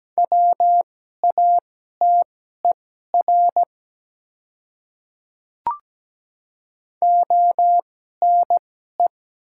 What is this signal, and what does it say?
Morse code, heard as a keyed tone, about 17 wpm, spaced as for 12 wpm, 700 Hz. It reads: WATER ONE
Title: 2x Character Spacing - Top 100 Words - 17wpm